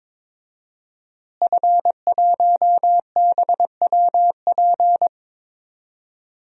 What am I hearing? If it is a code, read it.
Morse code, heard as a keyed tone, about 22 wpm, 700 Hz. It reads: F1BWP
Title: DXCC-100 Call Signs - 5 Characters - Repeated 2x - 22wpm